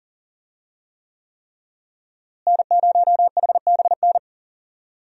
Morse code, heard as a keyed tone, about 40 wpm, 700 Hz. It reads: N0HBN
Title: USA Call Signs - 5 Characters - Repeated 3x - 40wpm